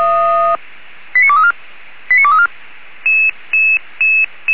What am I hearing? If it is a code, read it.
Five-tone selective call (ZVEI), heard as quick runs of five tones, tones of about 70 ms. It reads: 89234 89234